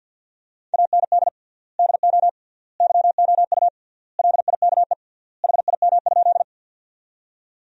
Morse code, heard as a keyed tone, about 50 wpm, 700 Hz. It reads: AND DO YOU LIKE HIM?
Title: 3x Word-Spacing - Sentences from Top 100 Words - 50wpm